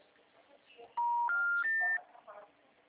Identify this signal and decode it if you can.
Special information tone.
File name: out- -2011- - - .483